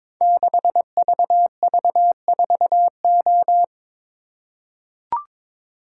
Morse code, heard as a keyed tone, about 22 wpm, 700 Hz. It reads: 6VV4O
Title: Code Groups - 5 Letter:Numbers - 22wpm